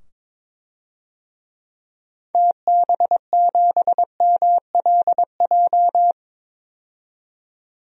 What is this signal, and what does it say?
Morse code, heard as a keyed tone, about 22 wpm, 700 Hz. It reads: TB7MLJ
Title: DXCC-100 Call Signs - 6 Characters - 22wpm